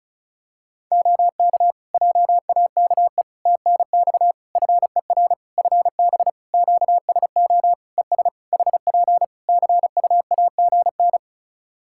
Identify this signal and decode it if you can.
Morse code, heard as a keyed tone, about 34 wpm, 700 Hz. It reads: OK JAKE TNX FER FB QSO ES HP CUAGN
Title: Practice QSO - Exchange 4 - Ending - 35wpm